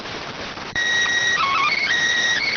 SSTV (slow-scan television), PD 290.